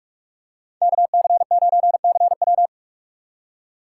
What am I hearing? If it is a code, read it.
Morse code, heard as a keyed tone, about 44 wpm, 700 Hz. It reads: KC9CW
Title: USA Call Signs - 5 Characters - 6x Character Spacing - 45wpm